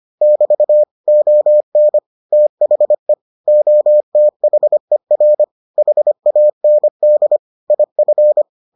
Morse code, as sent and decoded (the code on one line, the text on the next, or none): -...- / --- -. / - .... / --- - .... .-. / .... .- -. -.. / .. ..-.
<BT> ON THE OTHER HAND IF